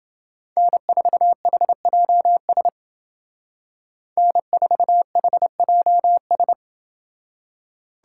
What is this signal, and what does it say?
Morse code, heard as a keyed tone, about 28 wpm, 700 Hz. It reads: N4HJS N4HJS